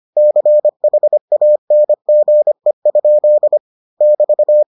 Morse code, heard as a keyed tone, about 25 wpm, 600 Hz. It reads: CHANGE? <BT>